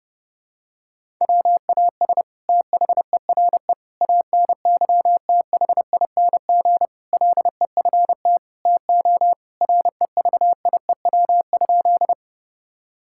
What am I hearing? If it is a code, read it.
Morse code, heard as a keyed tone, about 32 wpm, 700 Hz. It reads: WAS THERE ANYTHING LEFT TO REVIEW?
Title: Sentences from Top 1000 Words - Review Part 6 - 30wpm